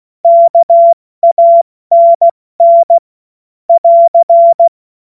Morse code, heard as a keyed tone, about 16 wpm, 680 Hz. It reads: KANN <AR>
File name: Morsekurs / Witze / Witz 29. Juni 2021, 70 F80